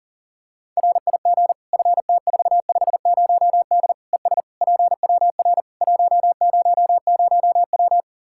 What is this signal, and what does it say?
Morse code, heard as a keyed tone, about 40 wpm, 700 Hz. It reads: RIG FT450D ES PWR 100W
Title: Practice QSO - Exchange 2 - Info - 40wpm